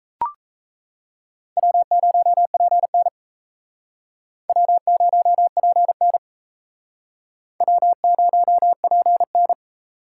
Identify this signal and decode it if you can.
Morse code, heard as a keyed tone, about 35 wpm, 700 Hz. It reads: W0PN W0PN W0PN
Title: Speed Racing - USA 1x2 Call Signs 28wpm